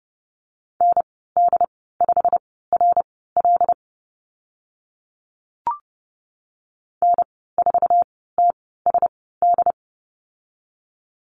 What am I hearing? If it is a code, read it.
Morse code, heard as a keyed tone, about 30 wpm, spaced as for 17 wpm, 690 Hz. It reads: ND5RL N4TSD
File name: Lesson 15g - Number 5 - Call Signs 30wpm